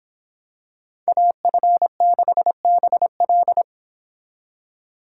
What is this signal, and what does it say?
Morse code, heard as a keyed tone, about 25 wpm, 700 Hz. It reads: AF6BL